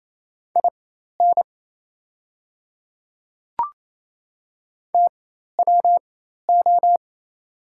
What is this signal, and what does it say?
Morse code, heard as a keyed tone, about 28 wpm, spaced as for 13 wpm, 700 Hz. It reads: IN TWO
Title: 4x Character Spacing - Top 100 Words - 28wpm